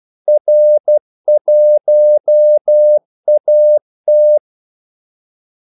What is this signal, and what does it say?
Morse code, heard as a keyed tone, about 12 wpm, 600 Hz. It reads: R1AT
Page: 2 ekstra ordmellemrum mellem kaldesignalerne